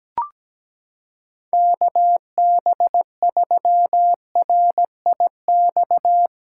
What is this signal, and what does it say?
Morse code, heard as a keyed tone, about 17 wpm, 700 Hz. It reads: KB3RIX
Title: USA Callsign DE Callsign - 6 Characters - 17wpm